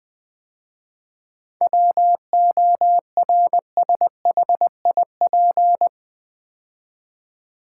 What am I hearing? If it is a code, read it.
Morse code, heard as a keyed tone, about 20 wpm, 700 Hz. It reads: WORSHIP
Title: 7 Letter Words - First 3 Letters Given - 20wpm